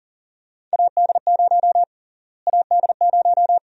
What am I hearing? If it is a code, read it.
Morse code, heard as a keyed tone, about 40 wpm, 700 Hz. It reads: AD0 AD0
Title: USA Call Sign Prefixes - Repeated 2x - 40wpm